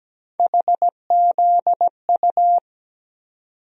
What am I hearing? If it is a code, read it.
Morse code, heard as a keyed tone, about 17 wpm, 700 Hz. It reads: HZU
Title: USA Call Sign Postfixes - 3 Letters 17wpm